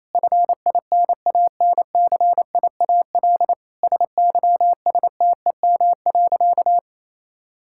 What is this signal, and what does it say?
Morse code, heard as a keyed tone, about 28 wpm, 700 Hz. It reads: FINANCIAL SYSTEM.
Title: Sentences from Top 1000 Words - Review Part 5 - 28wpm